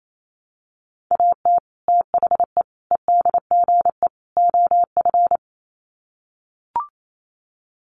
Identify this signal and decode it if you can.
Morse code, heard as a keyed tone, about 28 wpm, 700 Hz. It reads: AT THE EDGE OF
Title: Sets of 4 Words 28wpm